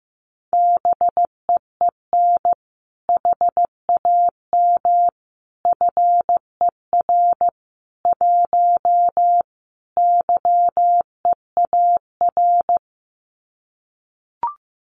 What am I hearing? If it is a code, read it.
Morse code, heard as a keyed tone, about 15 wpm, 700 Hz. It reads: BEEN HAM FER 1 YEAR
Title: Common QSO Phrases Encore 15wpm